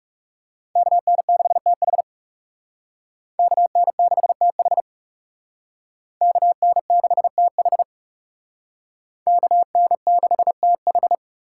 Morse code, heard as a keyed tone, about 38 wpm, 700 Hz. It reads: KN6TH KN6TH KN6TH KN6TH